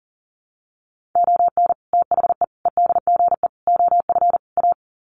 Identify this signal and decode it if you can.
Morse code, heard as a keyed tone, about 40 wpm, 700 Hz. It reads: ON THE EDGE OF A
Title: Sets of 5 Words Encore2 40wpm